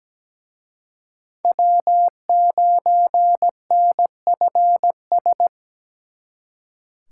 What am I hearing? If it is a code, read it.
Morse code, heard as a keyed tone, about 20 wpm, 700 Hz. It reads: W9NFS
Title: Speed Racing - USA 1x3 Call Signs 17wpm